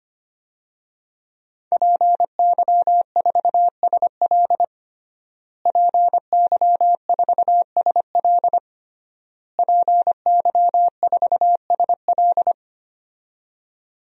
Morse code, heard as a keyed tone, about 25 wpm, 700 Hz. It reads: PY4SL PY4SL PY4SL
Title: DXCC-100 Call Signs - 5 Characters - Repeated 3x - 25wpm